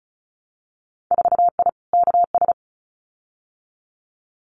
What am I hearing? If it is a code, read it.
Morse code, heard as a keyed tone, about 34 wpm, 700 Hz. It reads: 4I KS
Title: Winter Field Day Exchanges Encore 35wpm